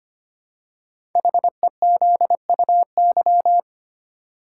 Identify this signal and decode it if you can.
Morse code, heard as a keyed tone, about 25 wpm, 700 Hz. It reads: HEZUY
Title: Code Groups - 5 Letters - 25wpm